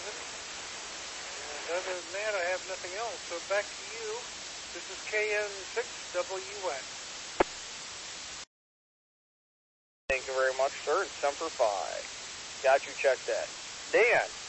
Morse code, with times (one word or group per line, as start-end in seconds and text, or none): none